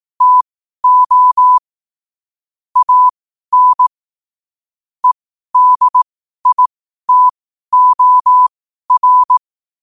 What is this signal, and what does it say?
Morse code, heard as a keyed tone, about 18 wpm, spaced as for 12 wpm, 1000 Hz. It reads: TO AN EDITOR
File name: qotdfiles / 12wpm